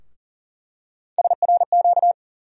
Morse code, heard as a keyed tone, about 39 wpm, 700 Hz. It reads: SRQ
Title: USA Call Sign Postfixes - 3 Letters 40wpm